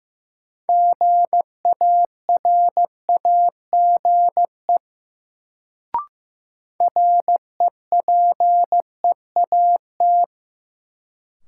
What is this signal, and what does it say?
Morse code, heard as a keyed tone, about 15 wpm, 700 Hz. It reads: GARAGE REPEAT
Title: Sentences from Top 1600 Words - New Word Drill Alt - 15wpm